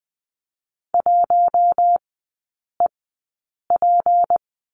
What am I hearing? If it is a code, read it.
Morse code, heard as a keyed tone, about 20 wpm, spaced as for 8 wpm, 700 Hz. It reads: 1EP
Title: 172 Lesson 28e - Number 9 - Three Characters 20wpm